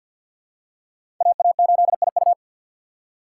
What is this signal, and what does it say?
Morse code, heard as a keyed tone, about 49 wpm, 690 Hz. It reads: AA8IU
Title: USA Call Signs - 5 Characters - 3x Character Spacing - 50wpm